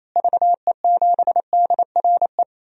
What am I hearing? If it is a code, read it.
Morse code, heard as a keyed tone, about 27 wpm, 700 Hz. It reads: VE7DRE
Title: DXCC-100 Call Signs - 6 Characters - Repeated 2x - 28wpm